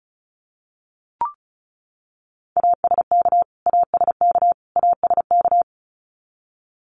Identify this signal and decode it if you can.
Morse code, heard as a keyed tone, about 34 wpm, 700 Hz. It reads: ASK ASK ASK
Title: Top 500 Words Repeated 35wpm